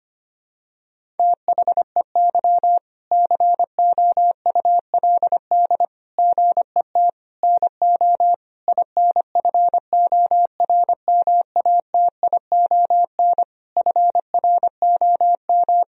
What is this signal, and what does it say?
Morse code, heard as a keyed tone, about 25 wpm, 700 Hz. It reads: THEY COULD GET NO INFORMATION FROM
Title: Sentences from Top 1000 Words - Review Part 3 - 25wpm